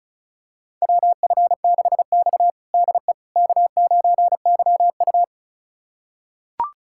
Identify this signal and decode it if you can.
Morse code, heard as a keyed tone, about 36 wpm, 700 Hz. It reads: WF6X DE K9YU
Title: USA Callsign DE Callsign - 4 Characters - 35wpm